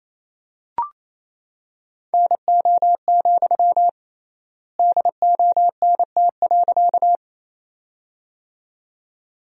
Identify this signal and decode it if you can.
Morse code, heard as a keyed tone, about 28 wpm, 700 Hz. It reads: NO, DONT.
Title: 3x Word-Spacing - Sentences from Top 300 Words - 28wpm